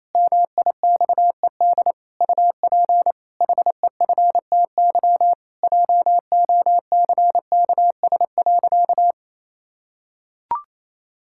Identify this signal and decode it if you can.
Morse code, heard as a keyed tone, about 28 wpm, 700 Hz. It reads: MIXED UP HEFTY JOCKS.
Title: Pangram Challenge v2 - 28wpm